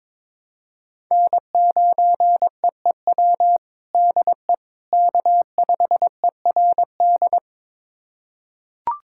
Morse code, heard as a keyed tone, about 22 wpm, 700 Hz. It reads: N9EEW DE K5ERD
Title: USA Callsign DE Callsign - 5 Characters - 22wpm